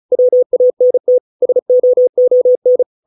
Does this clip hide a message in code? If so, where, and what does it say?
Yes: Morse code from 0.1 to 2.8 s, heard as a keyed tone, about 35 wpm, 500 Hz. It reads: WANT SOON